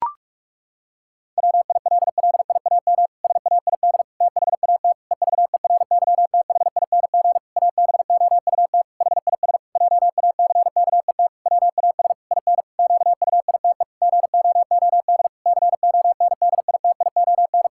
Morse code, heard as a keyed tone, about 44 wpm, 700 Hz. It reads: WILLIAM SAID THAT EVERYTHING ABOUT HIS JACKET WAS IN QUITE GOOD CONDITION
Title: Pangram Challenge v2 - 45wpm